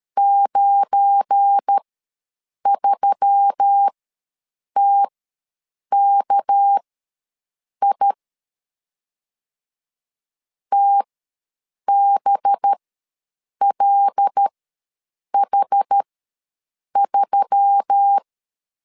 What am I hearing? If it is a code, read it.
Morse code, heard as a keyed tone, about 13 wpm, spaced as for 7 wpm, 800 Hz. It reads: 93TKI TBLH3
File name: LEZIONI / 7 GF9Z / 7 GF9 ws 12wpm 3344